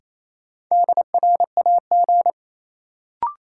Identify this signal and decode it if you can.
Morse code, heard as a keyed tone, about 28 wpm, 700 Hz. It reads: DRAG
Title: Sentences from Top 1500 Words - New Word Drill - 28wpm